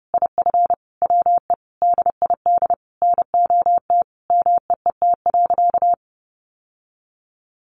Morse code, heard as a keyed tone, about 30 wpm, 700 Hz. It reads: IF WE DID NOT MEET.
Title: Sentences from Top 1100 Words 30wpm